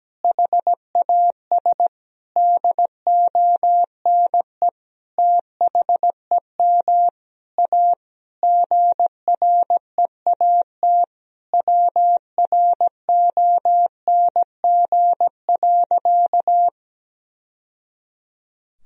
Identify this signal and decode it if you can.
Morse code, heard as a keyed tone, about 17 wpm, 700 Hz. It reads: HAS DONE THEM A GREAT WRONG.
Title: Sentences from Top 1000 Words - Review Part 4 - 17wpm